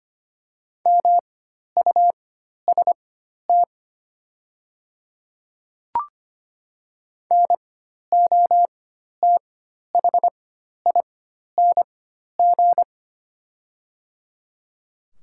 Morse code, heard as a keyed tone, about 25 wpm, spaced as for 12 wpm, 700 Hz. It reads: MUST NOTHING
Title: 4x Character Spacing - Top 500 Words - 25wpm